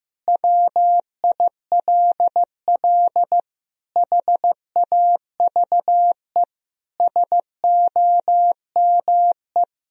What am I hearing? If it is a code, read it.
Morse code, heard as a keyed tone, about 15 wpm, 700 Hz. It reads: WILL HAVE SOME
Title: Sentences from Top 1000 Words - Review Part 4 - 15wpm